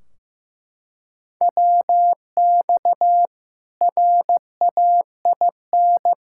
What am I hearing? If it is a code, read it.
Morse code, heard as a keyed tone, about 15 wpm, 700 Hz. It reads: WX RAIN
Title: Practice QSO - Exchange 2 - Info - 15wpm